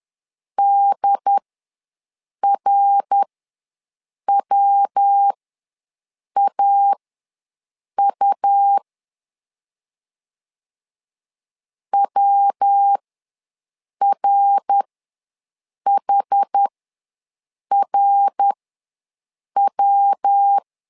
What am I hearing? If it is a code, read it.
Morse code, heard as a keyed tone, about 10 wpm, spaced as for 6 wpm, 800 Hz. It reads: DRWAU WRHRW
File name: LEZIONI / 3 MWRO / 3 MWR ws 10wpm 2151